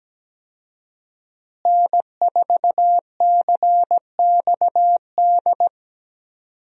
Morse code, heard as a keyed tone, about 17 wpm, 700 Hz. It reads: N4CXD
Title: USA Call Signs - 5 Characters - Repeated 2x - 17wpm